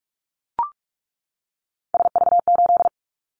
Morse code, heard as a keyed tone, about 46 wpm, 690 Hz. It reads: SV8
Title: Callsign Prefixes for Top 50 DXCC - 45wpm